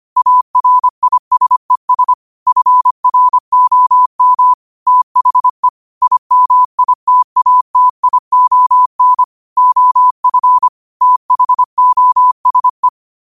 Morse code, heard as a keyed tone, about 25 wpm, 1000 Hz. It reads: ARISES FROM THE IMITATION OF THOSE